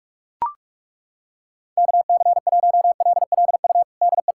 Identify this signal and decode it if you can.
Morse code, heard as a keyed tone, about 41 wpm, 700 Hz. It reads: KC1LLU DE
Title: USA Callsign DE Callsign 45wpm